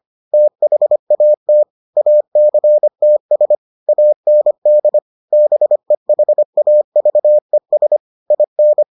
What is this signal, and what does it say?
Morse code, heard as a keyed tone, about 25 wpm, 600 Hz. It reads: THAT ACTS AND BEHAVES IN